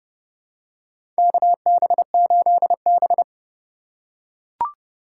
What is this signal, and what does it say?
Morse code, heard as a keyed tone, about 30 wpm, 700 Hz. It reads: KB8B
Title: USA Call Signs - 4 Characters - 2x Character Spacing - 30wpm